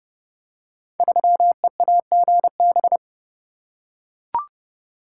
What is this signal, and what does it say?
Morse code, heard as a keyed tone, about 30 wpm, 700 Hz. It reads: 3EAGB
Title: Code Groups - 5 Letter:Numbers - 30wpm